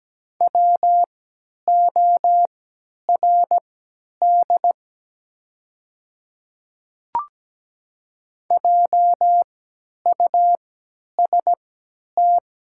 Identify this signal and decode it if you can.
Morse code, heard as a keyed tone, about 17 wpm, spaced as for 10 wpm, 700 Hz. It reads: WORD JUST
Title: 3x Character Spacing - Top 200 Words - 17wpm